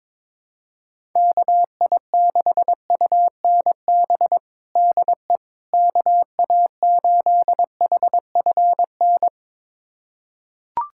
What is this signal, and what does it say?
Morse code, heard as a keyed tone, about 22 wpm, 700 Hz. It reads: KI6UNB DE KA8HFN
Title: USA Callsign DE Callsign - 6 Characters - 22wpm